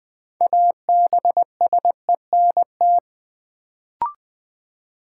Morse code, heard as a keyed tone, about 20 wpm, 700 Hz. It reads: ABSENT
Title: Build A Word - 2 Letters At A Time - 6-Letter Words - 20wpm